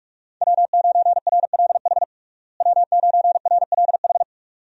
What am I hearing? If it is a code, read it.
Morse code, heard as a keyed tone, about 46 wpm, 700 Hz. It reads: W9RLH W9RLH
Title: USA Call Signs - 5 Characters - Repeated 2x - 45wpm